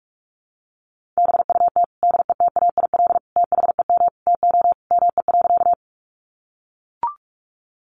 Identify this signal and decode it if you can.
Morse code, heard as a keyed tone, about 44 wpm, 700 Hz. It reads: BUT DETAIL THEM TO ME.
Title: Sentences from Top 1500 Words 45wpm